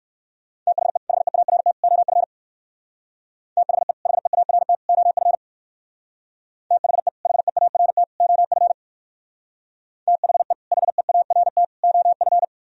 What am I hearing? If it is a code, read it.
Morse code, heard as a keyed tone, about 57 wpm, 700 Hz. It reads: THE HEART OF THE HEART OF THE HEART OF THE HEART OF